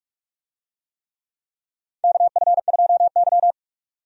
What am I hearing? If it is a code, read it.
Morse code, heard as a keyed tone, about 46 wpm, 700 Hz. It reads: KF2Y
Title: USA Call Signs - 4 Characters - Repeated 2x - 45wpm